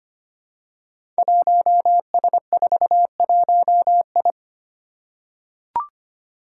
Morse code, heard as a keyed tone, about 25 wpm, 700 Hz. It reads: 1S41I
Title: Five Challenging Letter-Number Combinations 25wpm